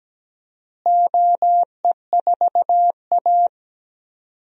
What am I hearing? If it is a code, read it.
Morse code, heard as a keyed tone, about 17 wpm, 700 Hz. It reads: OE4A
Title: DXCC-100 Call Signs - 4 Characters - Repeated 2x - 17wpm